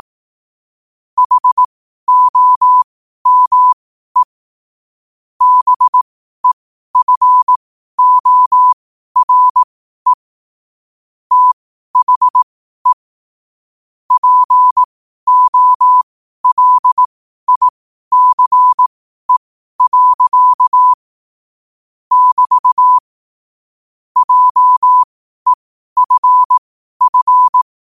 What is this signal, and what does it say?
Morse code, heard as a keyed tone, about 18 wpm, spaced as for 12 wpm, 1000 Hz. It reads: HOME BEFORE THE POLICE. <BT> JEFF
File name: qotdfiles / 12wpm